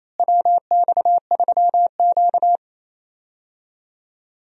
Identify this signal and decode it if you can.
Morse code, heard as a keyed tone, about 28 wpm, 700 Hz. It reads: WX3Q
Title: USA Call Signs - 4 Characters - Repeated 2x - 28wpm